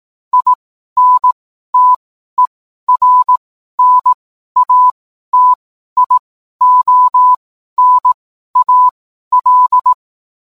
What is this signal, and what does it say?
Morse code, heard as a keyed tone, about 18 wpm, spaced as for 12 wpm, 1000 Hz. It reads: INTERNATIONAL
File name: cwblog / cw mp3 directory / mem morse 12